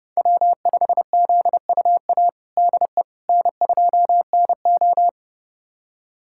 Morse code, heard as a keyed tone, about 30 wpm, 700 Hz. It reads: W5ZUA DE N2NO
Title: USA Callsign DE Callsign 30wpm